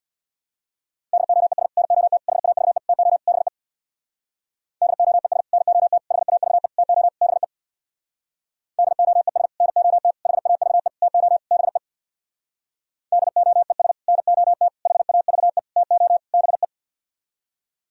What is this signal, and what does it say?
Morse code, heard as a keyed tone, about 64 wpm, 700 Hz. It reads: DOES NOT HAVE TO BE DOES NOT HAVE TO BE DOES NOT HAVE TO BE DOES NOT HAVE TO BE